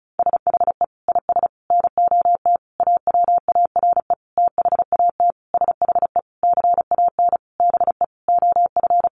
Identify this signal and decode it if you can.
Morse code, heard as a keyed tone, about 34 wpm, 700 Hz. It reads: SHE IS NOT AWARE THAT SHE CAN BE OF
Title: Sentences from Top 1200 Words 35wpm